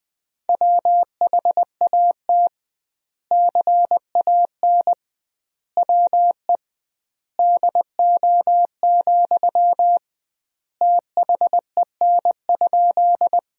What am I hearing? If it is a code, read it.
Morse code, heard as a keyed tone, about 21 wpm, 700 Hz. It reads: WHAT CAN WE DO, THEN?
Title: 2x Word-Spacing - Sentences from Top 100 Words - 20wpm